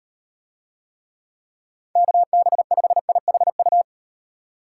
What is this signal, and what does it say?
Morse code, heard as a keyed tone, about 35 wpm, 700 Hz. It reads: KB5IHU